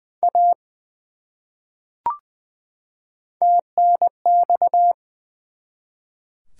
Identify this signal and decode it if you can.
Morse code, heard as a keyed tone, about 20 wpm, 700 Hz. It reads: A TNX
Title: Top 180 QSO Elements 20wpm